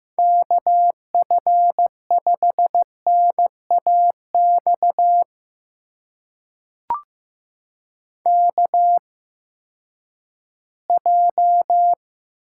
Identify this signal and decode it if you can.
Morse code, heard as a keyed tone, about 15 wpm, spaced as for 10 wpm, 700 Hz. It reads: KF5NAX KJ
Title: USA Call Signs - 6 Characters - 8x Character Spacing - 15wpm